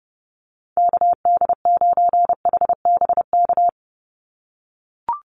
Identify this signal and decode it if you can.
Morse code, heard as a keyed tone, about 30 wpm, 700 Hz. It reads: KD9HBK
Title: USA Call Signs 30wpm